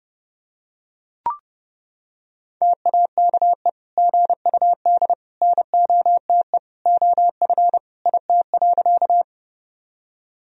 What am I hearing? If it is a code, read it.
Morse code, heard as a keyed tone, about 30 wpm, 700 Hz. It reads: TAKE GUD NOTE OF IT.
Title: Sentences from Top 500 Words and Abbreviations 30wpm